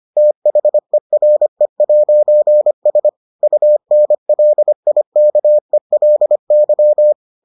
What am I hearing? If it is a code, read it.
Morse code, heard as a keyed tone, about 25 wpm, 600 Hz. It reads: THERE'S UNLIKELY